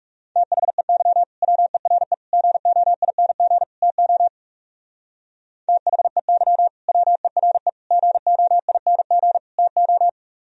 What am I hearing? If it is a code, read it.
Morse code, heard as a keyed tone, about 40 wpm, 700 Hz. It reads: THEY WERE GOING TO THEY WERE GOING TO